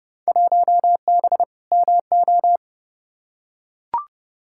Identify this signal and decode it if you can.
Morse code, heard as a keyed tone, about 30 wpm, 700 Hz. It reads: 1B MO
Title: Field Day Exchanges - 30wpm